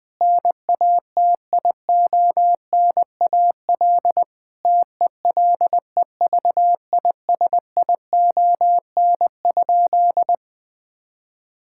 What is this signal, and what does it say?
Morse code, heard as a keyed tone, about 20 wpm, 700 Hz. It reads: NATIONAL TELEVISION?
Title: Sentences from Top 1000 Words - Review Part 5 - 20wpm